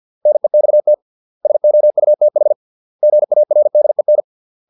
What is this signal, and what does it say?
Morse code, heard as a keyed tone, about 50 wpm, 600 Hz. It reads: NEXT SOUTH GARDEN